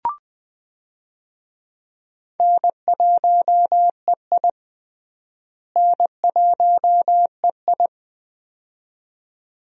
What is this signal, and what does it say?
Morse code, heard as a keyed tone, about 20 wpm, 700 Hz. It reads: N1EI N1EI
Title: USA Call Signs - 4 Characters - Repeated 2x - 20wpm